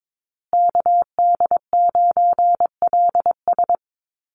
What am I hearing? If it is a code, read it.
Morse code, heard as a keyed tone, about 22 wpm, 700 Hz. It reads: KD9LS
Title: USA Call Signs Repeated 2x 22wpm